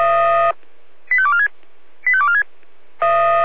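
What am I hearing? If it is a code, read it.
Five-tone selective call (ZVEI), heard as quick runs of five tones, tones of about 70 ms. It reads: 85246 85246
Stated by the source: ZVEI five-tone call received by SDR.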